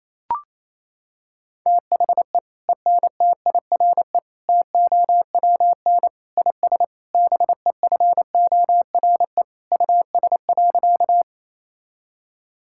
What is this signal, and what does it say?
Morse code, heard as a keyed tone, about 28 wpm, 700 Hz. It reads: THE ENTIRE TOWN IS BEFORE US.
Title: Sentences from Top 1000 Words - Review Part 5 - 28wpm